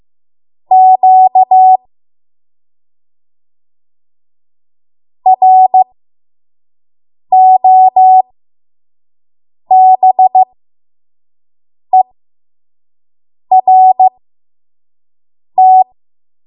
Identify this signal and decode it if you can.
Morse code, heard as a keyed tone, about 15 wpm, spaced as for 5 wpm, 750 Hz. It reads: Q ROBERT
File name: Morse / 05WPM